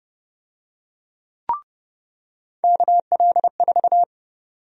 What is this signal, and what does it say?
Morse code, heard as a keyed tone, about 30 wpm, 700 Hz. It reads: KL4
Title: USA Call Sign Prefixes - 30wpm